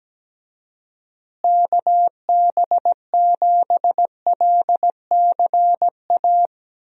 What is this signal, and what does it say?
Morse code, heard as a keyed tone, about 17 wpm, 700 Hz. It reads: KB7LCA
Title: USA Call Signs - 6 Characters - Repeated 2x - 17wpm